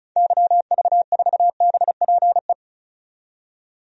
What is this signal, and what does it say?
Morse code, heard as a keyed tone, about 36 wpm, 700 Hz. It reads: YV4BPE
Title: DXCC-100 Call Signs - 6 Characters - 35wpm